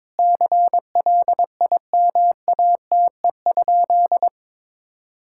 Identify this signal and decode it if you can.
Morse code, heard as a keyed tone, about 21 wpm, 700 Hz. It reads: CLIMATE?
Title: Sentences from Top 1600 Words 22wpm